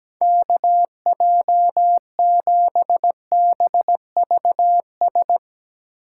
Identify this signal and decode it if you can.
Morse code, heard as a keyed tone, about 17 wpm, 700 Hz. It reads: KJ7BVS
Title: USA Call Signs - 6 Characters - 4x Character Spacing - 17wpm